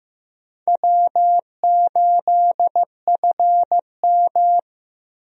Morse code, heard as a keyed tone, about 16 wpm, 700 Hz. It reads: W8FM
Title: USA Call Signs - 4 Characters - Repeated 2x - 15wpm